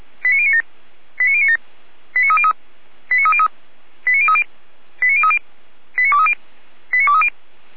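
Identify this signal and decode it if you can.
Five-tone selective call (ZVEI), heard as quick runs of five tones, tones of about 70 ms. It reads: 89097 89097 89303 89303 89030 89030 89230 89230
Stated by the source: Five-tone ZVEI selective call received on SDR.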